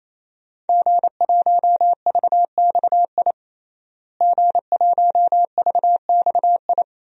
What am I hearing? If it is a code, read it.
Morse code, heard as a keyed tone, about 28 wpm, 700 Hz. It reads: G1VXI G1VXI
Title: DXCC-100 Call Signs - 5 Characters - Repeated 3x - 28wpm